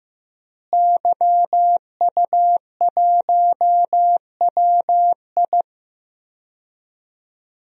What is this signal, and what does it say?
Morse code, heard as a keyed tone, about 16 wpm, 700 Hz. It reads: YU1WI
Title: DXCC-100 Call Signs - 5 Characters - Repeated 2x - 15wpm